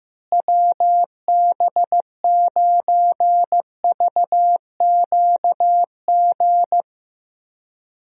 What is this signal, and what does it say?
Morse code, heard as a keyed tone, about 15 wpm, 700 Hz. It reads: WB9VQG
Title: USA Call Signs - 6 Characters - Repeated 2x - 15wpm